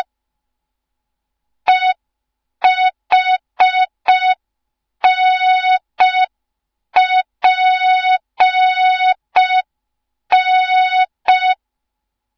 Morse code, heard as a keyed tone, about 5 wpm, 3700 Hz. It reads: EHNPN
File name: Morze-HA2MN / hang / 07 GYAKORLAT